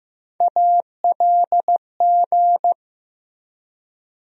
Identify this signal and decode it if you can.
Morse code, heard as a keyed tone, about 15 wpm, 700 Hz. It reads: ALG
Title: USA Call Sign Postfixes - 3 Letters 15wpm